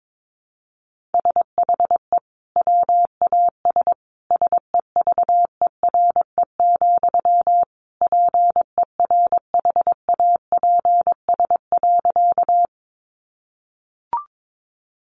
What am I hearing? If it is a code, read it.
Morse code, heard as a keyed tone, about 22 wpm, 700 Hz. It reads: SHE WAS SEVERE, PERHAPS.
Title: Sentences from Top 1400 Words 22wpm